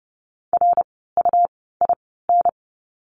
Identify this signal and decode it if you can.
Morse code, heard as a keyed tone, about 30 wpm, spaced as for 17 wpm, 700 Hz. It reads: RUIN
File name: Lesson 17f - Letter u - Words 30wpm